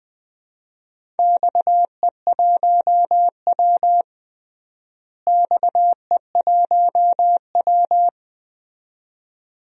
Morse code, heard as a keyed tone, about 21 wpm, 700 Hz. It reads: XE1W XE1W
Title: DXCC-100 Call Signs - 4 Characters - Repeated 2x - 20wpm